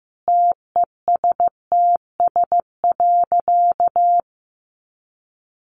Morse code, heard as a keyed tone, about 15 wpm, 700 Hz. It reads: TESTS.
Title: Additional Easy Phrases 15wpm